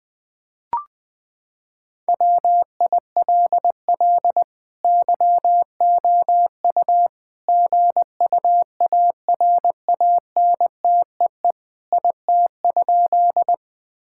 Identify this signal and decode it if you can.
Morse code, heard as a keyed tone, about 21 wpm, 700 Hz. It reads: WILL YOU GUARANTEE IT?
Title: Sentences from Top 1600 Words 20wpm